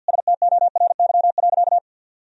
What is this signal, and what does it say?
Morse code, heard as a keyed tone, about 52 wpm, 700 Hz. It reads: STORY.
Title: Sentences from Top 1000 Words - Review Part 1 - 50wpm